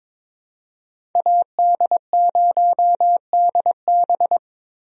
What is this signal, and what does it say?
Morse code, heard as a keyed tone, about 22 wpm, 700 Hz. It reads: AD0DB
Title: USA Call Signs - 5 Characters - Repeated 2x - 22wpm